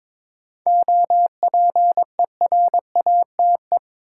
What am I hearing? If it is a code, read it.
Morse code, heard as a keyed tone, about 22 wpm, 700 Hz. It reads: OPERATE
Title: Sentences from Top 1500 Words - New Word Drill - 22wpm